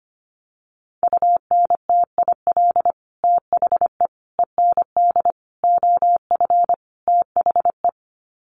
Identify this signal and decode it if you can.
Morse code, heard as a keyed tone, about 25 wpm, 700 Hz. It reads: UNTIL THE END OF THE
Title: Sets of 5 Words Encore 25wpm